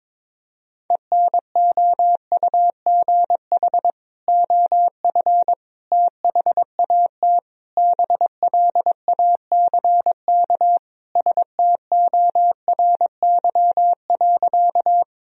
Morse code, heard as a keyed tone, about 22 wpm, 700 Hz. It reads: ENOUGH OF THAT BLACK STORY.
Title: Sentences from Top 1000 Words - Review Part 2 - 22wpm